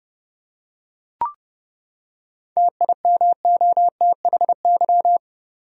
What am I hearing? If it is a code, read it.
Morse code, heard as a keyed tone, about 30 wpm, 700 Hz. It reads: TIMOTHY
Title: USA Common Names - 30wpm